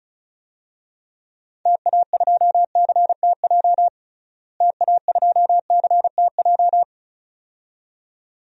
Morse code, heard as a keyed tone, about 34 wpm, 700 Hz. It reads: TA2CTJ TA2CTJ
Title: DXCC-100 Call Signs - 6 Characters - Repeated 2x - 35wpm